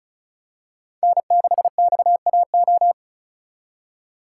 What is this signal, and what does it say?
Morse code, heard as a keyed tone, about 36 wpm, 700 Hz. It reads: N6XAO
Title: USA Call Signs - 5 Characters - 6x Character Spacing - 35wpm